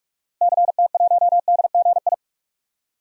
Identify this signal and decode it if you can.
Morse code, heard as a keyed tone, about 46 wpm, 700 Hz. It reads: CT1DGI
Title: DXCC-100 Call Signs - 6 Characters - Repeated 2x - 45wpm